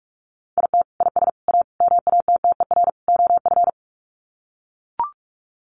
Morse code, heard as a keyed tone, about 44 wpm, 700 Hz. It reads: IT IS A MATTER OF
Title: Sets of 5 Words Encore2 45wpm